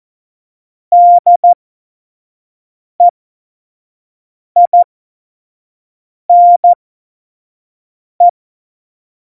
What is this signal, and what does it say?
Morse code, heard as a keyed tone, about 14 wpm, spaced as for 5 wpm, 700 Hz. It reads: DEINE
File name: Morsekurs / Witze / Witz 12. September 17, Tempo 25